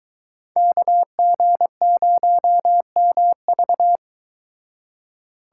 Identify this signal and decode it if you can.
Morse code, heard as a keyed tone, about 22 wpm, 700 Hz. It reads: KG0MV